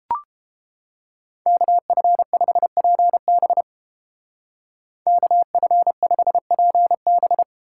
Morse code, heard as a keyed tone, about 29 wpm, 700 Hz. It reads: KF5PB KF5PB